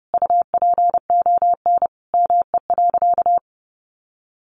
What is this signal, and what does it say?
Morse code, heard as a keyed tone, about 30 wpm, 700 Hz. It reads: UPON ME.
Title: Sentences from Top 1100 Words 30wpm